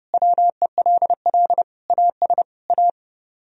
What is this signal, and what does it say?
Morse code, heard as a keyed tone, about 28 wpm, 700 Hz. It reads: WELL AS A